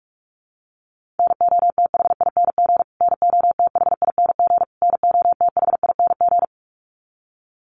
Morse code, heard as a keyed tone, about 44 wpm, 700 Hz. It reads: NOTHING NOTHING NOTHING
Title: Top 500 Words Repeated 45wpm